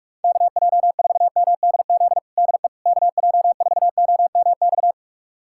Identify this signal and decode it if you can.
Morse code, heard as a keyed tone, about 46 wpm, 700 Hz. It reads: KJ4MDZ DE KJ4OMX
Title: USA Callsign DE Callsign - 6 Characters - 45wpm